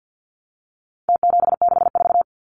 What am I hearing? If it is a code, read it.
Morse code, heard as a keyed tone, about 50 wpm, 700 Hz. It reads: T764
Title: Numbers - Contest Style - 4-Digit 50wpm